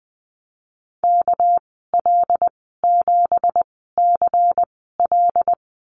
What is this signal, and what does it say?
Morse code, heard as a keyed tone, about 20 wpm, spaced as for 15 wpm, 700 Hz. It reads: KL7CL
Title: 207 Lesson 33g - Letter j - Call Signs 20wpm